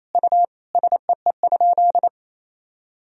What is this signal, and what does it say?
Morse code, heard as a keyed tone, about 28 wpm, 700 Hz. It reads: U SEE?
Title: Sentences from Top 100 Words and Abbreviations 28wpm